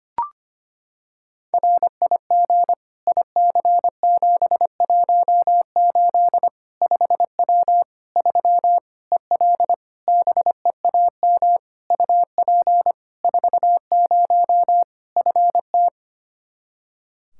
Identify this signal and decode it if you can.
Morse code, heard as a keyed tone, about 25 wpm, 700 Hz. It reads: RIG IC718 5W 3 EL BEAM UP 40 FT
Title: K1USN SST QSO Practice 25wpm